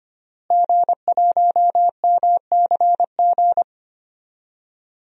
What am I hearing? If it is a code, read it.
Morse code, heard as a keyed tone, about 25 wpm, 700 Hz. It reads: G1MCG
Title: DXCC-100 Call Signs - 5 Characters - Repeated 3x - 25wpm